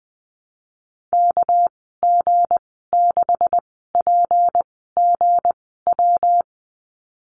Morse code, heard as a keyed tone, about 20 wpm, spaced as for 14 wpm, 700 Hz. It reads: KG6PGW
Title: 245 Lesson 39g - Letter z - Call Signs 20wpm